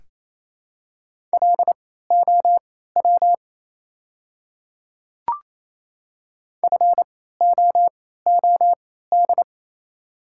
Morse code, heard as a keyed tone, about 28 wpm, spaced as for 16 wpm, 700 Hz. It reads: LOW FOOD
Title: 3x Character Spacing - Top 300 Words - 28wpm